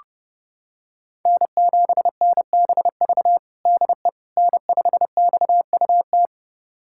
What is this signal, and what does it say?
Morse code, heard as a keyed tone, about 32 wpm, 700 Hz. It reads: N7NBV DE N5XUT
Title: USA Callsign DE Callsign - 5 Characters - 30wpm